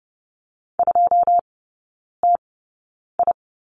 Morse code, heard as a keyed tone, about 30 wpm, spaced as for 9 wpm, 700 Hz. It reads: 2TI
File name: 081 Lesson 14e - Number 2 - Three Characters 30wpm